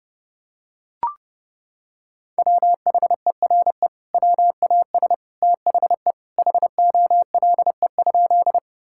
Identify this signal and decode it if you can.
Morse code, heard as a keyed tone, about 30 wpm, 700 Hz. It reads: WHERE WAS THE HOLE?
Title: Sentences from Top 1600 Words 30wpm